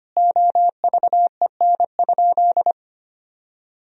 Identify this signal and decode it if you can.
Morse code, heard as a keyed tone, about 25 wpm, 700 Hz. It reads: OVEN?
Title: Sentences from Top 1000 Words - Review Part 4 - 25wpm